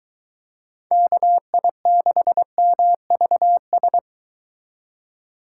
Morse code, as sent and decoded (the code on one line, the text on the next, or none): -.- .. -.... -- ...- ...
KI6MVS